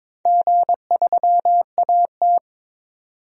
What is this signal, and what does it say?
Morse code, heard as a keyed tone, about 21 wpm, 700 Hz. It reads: G3AT
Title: DXCC-100 Call Signs - 4 Characters - 22wpm